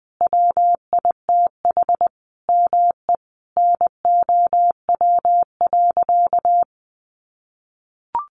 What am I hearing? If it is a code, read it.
Morse code, heard as a keyed tone, about 20 wpm, 700 Hz. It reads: WITH ME NOW.
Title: Sentences from Top 1500 Words 20wpm